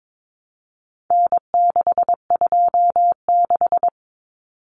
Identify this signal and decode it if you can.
Morse code, heard as a keyed tone, about 22 wpm, 700 Hz. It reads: N626
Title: Common Cut Numbers - 4-Digit 22wpm